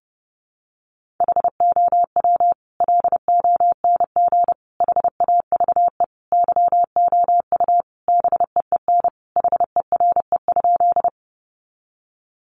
Morse code, heard as a keyed tone, about 30 wpm, 700 Hz. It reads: HOW LONG HAVE YOU BEEN HERE?
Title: Sentences from Top 200 Words 30wpm